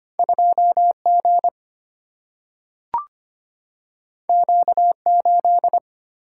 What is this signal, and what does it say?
Morse code, heard as a keyed tone, about 25 wpm, 700 Hz. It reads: 2 G Q 8
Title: Single-Letter Single-Number 25wpm